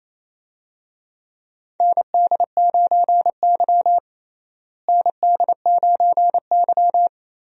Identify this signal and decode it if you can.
Morse code, heard as a keyed tone, about 28 wpm, 700 Hz. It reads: ND9Y ND9Y
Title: USA Call Signs - 4 Characters - Repeated 3x - 28wpm